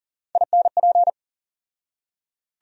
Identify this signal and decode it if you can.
Morse code, heard as a keyed tone, about 40 wpm, 700 Hz. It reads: INP
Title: USA Call Sign Postfixes - 3 Letters 40wpm